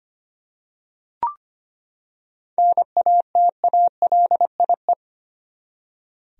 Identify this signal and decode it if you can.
Morse code, heard as a keyed tone, about 25 wpm, 700 Hz. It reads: NATALIE
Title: USA Common Names - 25wpm